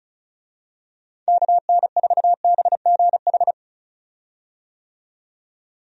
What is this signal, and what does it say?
Morse code, heard as a keyed tone, about 36 wpm, 700 Hz. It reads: KN4BGH
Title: USA Call Signs - 6 Characters - 35wpm